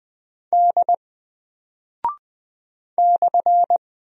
Morse code, heard as a keyed tone, about 20 wpm, 700 Hz. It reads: D /
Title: Single Character (Letter:Number:Punctuation:Prosign) 20wpm